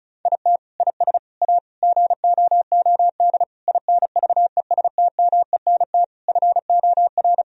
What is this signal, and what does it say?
Morse code, heard as a keyed tone, about 34 wpm, 700 Hz. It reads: IT IS A GOOD INVESTMENT FOR
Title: Sentences from Top 1000 Words - Review Part 4 - 35wpm